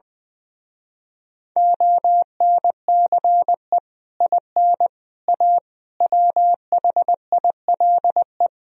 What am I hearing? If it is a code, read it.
Morse code, heard as a keyed tone, about 20 wpm, 700 Hz. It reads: ONCE IN A WHILE
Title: Speed-Racing - Sets of 4 Words 20wpm